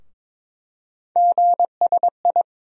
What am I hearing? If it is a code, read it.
Morse code, heard as a keyed tone, about 22 wpm, 700 Hz. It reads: GSI
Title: USA Call Sign Postfixes - 3 Letters 22wpm